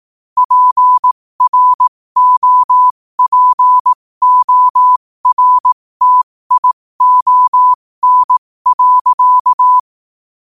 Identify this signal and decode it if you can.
Morse code, heard as a keyed tone, about 18 wpm, spaced as for 15 wpm, 1000 Hz. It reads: PROPORTION.
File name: qotdfiles / 15wpm